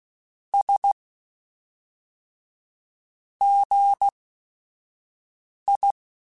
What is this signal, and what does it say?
Morse code, heard as a keyed tone, about 16 wpm, spaced as for 5 wpm, 800 Hz. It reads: S GI